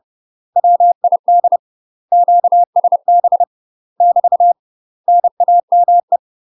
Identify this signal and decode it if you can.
Morse code, heard as a keyed tone, about 30 wpm, 700 Hz. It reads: WID QSB <BT> NAME
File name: UK MASTER QSOs TRAINING / 30WPM / UK MASTER QSOs 0003